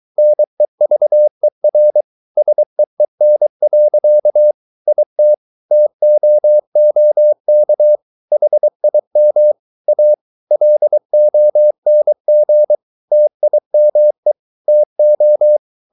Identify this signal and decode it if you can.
Morse code, heard as a keyed tone, about 23 wpm, 600 Hz. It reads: NEVER SEEN. IT TOOK HIM A LONG TIME TO